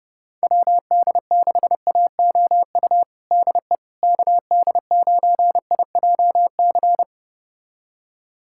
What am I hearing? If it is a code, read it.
Morse code, heard as a keyed tone, about 30 wpm, 700 Hz. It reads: WD6AOU DE KD9IJC
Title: USA Callsign DE Callsign - 6 Characters - 30wpm